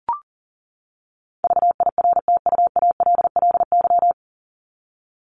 Morse code, heard as a keyed tone, about 40 wpm, 700 Hz. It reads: VIRTUALLY
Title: LY Suffix - Top 100 Words with Base Words 40wpm